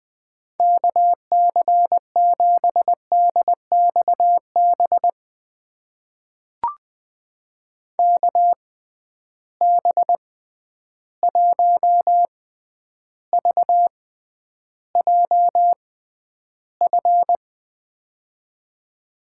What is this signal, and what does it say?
Morse code, heard as a keyed tone, about 20 wpm, spaced as for 10 wpm, 700 Hz. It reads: KC7DXB KB1VJF
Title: USA Call Signs - 6 Characters - 6x Character Spacing - 20wpm